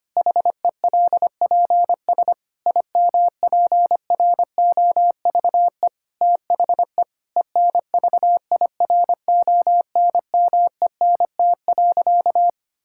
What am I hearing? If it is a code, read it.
Morse code, heard as a keyed tone, about 25 wpm, 700 Hz. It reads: HELPS IMPROVE THE ENVIRONMENT.
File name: practice / 25wpm / Difficult Sentences with Hints - 25wpm